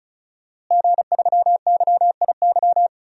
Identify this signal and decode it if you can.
Morse code, heard as a keyed tone, about 34 wpm, 700 Hz. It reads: G3YIY
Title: DXCC-100 Call Signs - 5 Characters - Repeated 2x - 35wpm